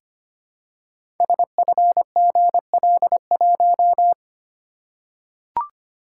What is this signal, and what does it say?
Morse code, heard as a keyed tone, about 25 wpm, 700 Hz. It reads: SFGL1
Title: Code Groups - 5 Letter:Numbers - 25wpm